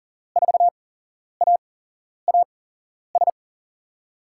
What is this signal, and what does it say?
Morse code, heard as a keyed tone, about 40 wpm, spaced as for 11 wpm, 700 Hz. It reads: 4AAS
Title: USA Call Signs - 6 Characters - 8x Character Spacing - 40wpm